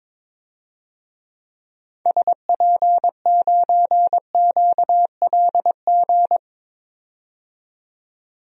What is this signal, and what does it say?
Morse code, heard as a keyed tone, about 22 wpm, 700 Hz. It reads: SP9QLG
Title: DXCC-100 Call Signs - 6 Characters - 22wpm